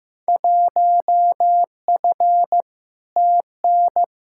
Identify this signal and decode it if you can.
Morse code, heard as a keyed tone, about 15 wpm, 700 Hz. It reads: 1F TN
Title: Field Day Exchanges - 15wpm